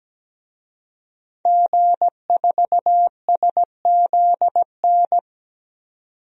Morse code, heard as a keyed tone, about 17 wpm, 700 Hz. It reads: G4SZN
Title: DXCC-100 Call Signs - 5 Characters - 17wpm